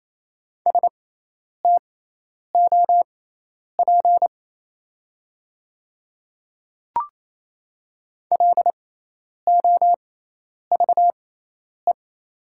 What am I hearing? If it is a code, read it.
Morse code, heard as a keyed tone, about 27 wpm, spaced as for 10 wpm, 700 Hz. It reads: STOP LOVE
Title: 6x Character Spacing - Top 500 Words - 28wpm